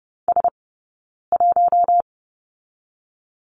Morse code, heard as a keyed tone, about 30 wpm, 700 Hz. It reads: S 1
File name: course / 30:10wpm / 099 Lesson 17d - Letter u - Two Characters 30wpm